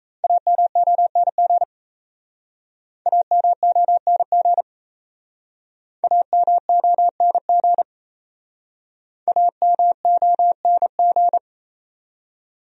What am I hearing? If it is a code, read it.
Morse code, heard as a keyed tone, about 36 wpm, 700 Hz. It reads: AMONG AMONG AMONG AMONG